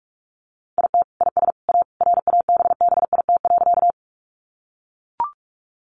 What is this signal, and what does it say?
Morse code, heard as a keyed tone, about 44 wpm, 700 Hz. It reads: IT IS A RABBIT.
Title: Easy Phrases 45wpm